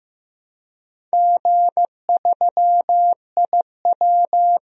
Morse code, heard as a keyed tone, about 15 wpm, 700 Hz. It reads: G3IW
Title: DXCC-100 Call Signs - 4 Characters - Repeated 3x - 15wpm